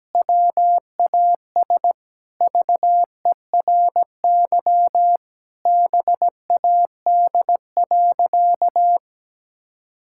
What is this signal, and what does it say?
Morse code, heard as a keyed tone, about 17 wpm, 700 Hz. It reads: WAS VERY BAD.
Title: Sentences from Top 1000 Words - Review Part 5 - 17wpm